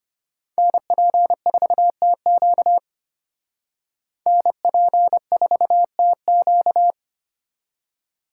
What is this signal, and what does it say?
Morse code, heard as a keyed tone, about 30 wpm, 700 Hz. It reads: NP4TQ NP4TQ
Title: Speed Racing - USA 2x2 Call Signs 25wpm